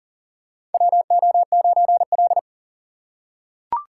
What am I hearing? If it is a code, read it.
Morse code, heard as a keyed tone, about 40 wpm, 700 Hz. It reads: WO9L
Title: USA Call Signs - 4 Characters - 4x Character Spacing - 40wpm